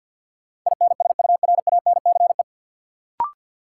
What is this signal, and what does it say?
Morse code, heard as a keyed tone, about 45 wpm, 700 Hz. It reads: INSURANCE
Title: Sentences from Top 1100 Words - New Word Drill Alt - 50wpm